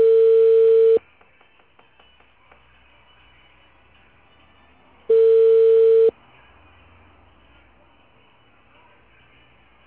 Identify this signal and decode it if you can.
Ringback tone (China, 450 Hz).